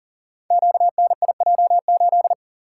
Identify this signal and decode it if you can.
Morse code, heard as a keyed tone, about 41 wpm, 700 Hz. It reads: QNIJ8
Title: Code Groups - 5 Letter:Numbers - 40wpm